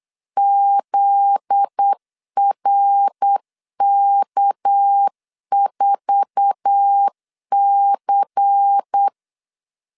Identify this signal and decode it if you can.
Morse code, heard as a keyed tone, about 8 wpm, 800 Hz. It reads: ZRK4C
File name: LEZIONI / 10 SIGNS / 10A SLASH 8wpm 3452